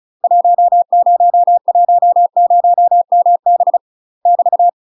Morse code, heard as a keyed tone, about 34 wpm, 700 Hz. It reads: 1010MB <BT>
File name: UK MASTER QSOs TRAINING / 35WPM / UK MASTER QSOs 0000